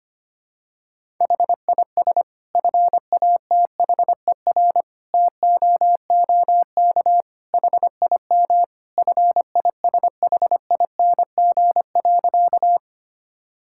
Morse code, heard as a keyed tone, about 25 wpm, 700 Hz. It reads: HIS FATHER TOOK HIM FISHING.
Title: Sentences from Top 1000 Words - Review Part 5 - 25wpm